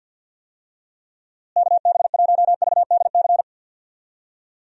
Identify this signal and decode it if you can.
Morse code, heard as a keyed tone, about 50 wpm, 700 Hz. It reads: KB1VDC
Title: USA Call Signs - 6 Characters - Repeated 3x - 50wpm